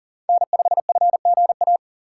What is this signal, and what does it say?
Morse code, heard as a keyed tone, about 40 wpm, 700 Hz. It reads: N5FGA
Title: USA Call Signs - 5 Characters - 3x Character Spacing - 40wpm